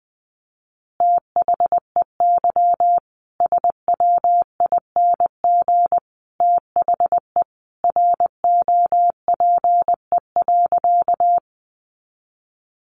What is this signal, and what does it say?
Morse code, heard as a keyed tone, about 20 wpm, 700 Hz. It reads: THEY SWING THE ROPE.
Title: Easy Phrases 20wpm